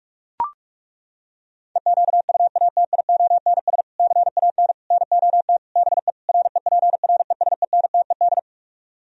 Morse code, heard as a keyed tone, about 40 wpm, 700 Hz. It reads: EQUATIONS CAN NOT BE REPRESENTED